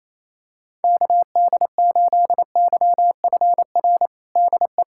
Morse code, heard as a keyed tone, about 28 wpm, 700 Hz. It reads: KD8YFR DE
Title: USA Callsign DE Callsign 28wpm